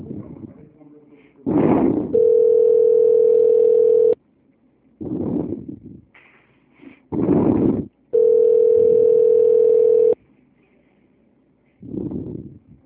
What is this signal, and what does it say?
Ringback tone (North America), 440 + 480 Hz, with speech.
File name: out- -2004- - - .50